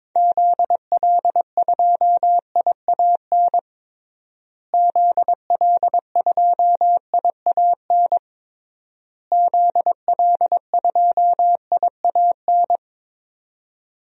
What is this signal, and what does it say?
Morse code, heard as a keyed tone, about 22 wpm, 700 Hz. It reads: ZL2IAN ZL2IAN ZL2IAN
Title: DXCC-100 Call Signs - 6 Characters - Repeated 3x - 22wpm